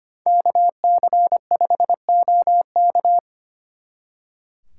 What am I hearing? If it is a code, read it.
Morse code, heard as a keyed tone, about 30 wpm, 700 Hz. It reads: KC5OK
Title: Speed Racing - USA 2x2 Call Signs 25wpm